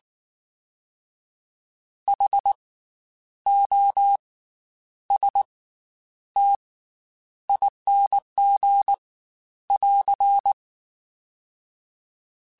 Morse code, heard as a keyed tone, about 19 wpm, spaced as for 9 wpm, 800 Hz. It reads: HOSTING <AR>
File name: Special INGs / Special INGs-mp3s / Slow ing @18